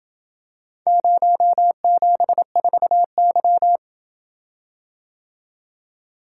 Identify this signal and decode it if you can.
Morse code, heard as a keyed tone, about 27 wpm, 700 Hz. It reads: 074Y